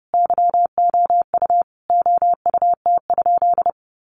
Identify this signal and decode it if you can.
Morse code, heard as a keyed tone, about 30 wpm, 700 Hz. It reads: YOU OUT?
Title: Sentences from Top 1300 Words 30wpm